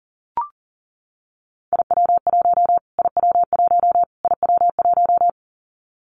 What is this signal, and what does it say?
Morse code, heard as a keyed tone, about 40 wpm, 700 Hz. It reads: IW1 IW1 IW1
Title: Callsign Prefixes for Top 25 DXCC - Repeated 3x - 40wpm